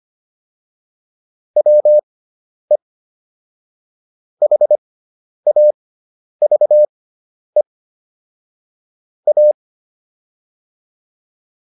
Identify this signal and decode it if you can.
Morse code, heard as a keyed tone, about 25 wpm, spaced as for 10 wpm, 600 Hz. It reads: WE HAVE A